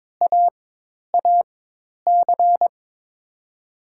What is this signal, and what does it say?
Morse code, heard as a keyed tone, about 22 wpm, spaced as for 10 wpm, 700 Hz. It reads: AAC
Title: USA Call Signs - 4x Character Spacing - 22wpm